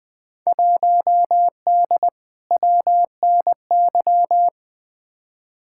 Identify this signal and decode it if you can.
Morse code, heard as a keyed tone, about 20 wpm, 700 Hz. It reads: 1D WNY
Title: Field Day Exchanges - 20wpm